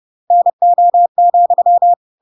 Morse code, heard as a keyed tone, about 30 wpm, 700 Hz. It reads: NO,
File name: Pinocchio / Pinocchio ch1 30 30 ik1hge